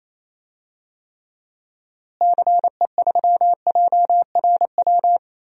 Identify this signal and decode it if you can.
Morse code, heard as a keyed tone, about 28 wpm, 700 Hz. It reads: CE3JRW
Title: DXCC-100 Call Signs - 6 Characters - Repeated 2x - 28wpm